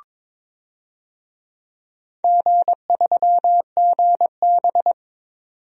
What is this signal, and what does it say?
Morse code, heard as a keyed tone, about 22 wpm, 700 Hz. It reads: G3GB
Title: DXCC-100 Call Signs - 4 Characters - 22wpm